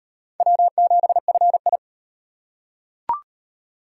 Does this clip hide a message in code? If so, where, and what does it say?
Yes: Morse code from 0.4 to 1.8 s, heard as a keyed tone, about 34 wpm, 700 Hz. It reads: W7FI